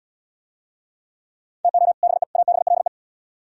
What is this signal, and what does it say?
Morse code, heard as a keyed tone, about 76 wpm, 700 Hz. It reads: TO BE ABLE